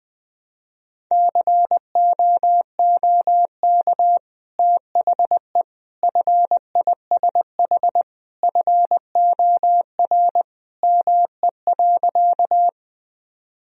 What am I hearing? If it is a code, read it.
Morse code, heard as a keyed tone, about 20 wpm, 700 Hz. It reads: COOK THE FISH FOR ME.
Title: Sentences from Top 1000 Words - Review Part 4 - 20wpm